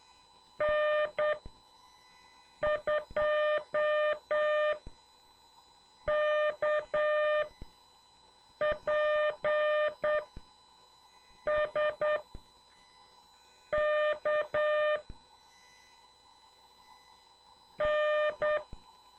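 Morse code, heard as a keyed tone, about 8 wpm, spaced as for 4 wpm, 600 Hz. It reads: N2KPS KN
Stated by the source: CW class lesson from The Martian Net with instructor N2KPS.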